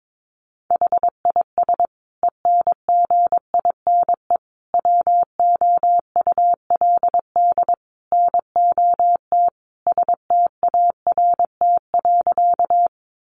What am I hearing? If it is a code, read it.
Morse code, heard as a keyed tone, about 22 wpm, 700 Hz. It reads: HIS ENGINE WOULD NOT START.
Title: Sentences from Top 1500 Words 22wpm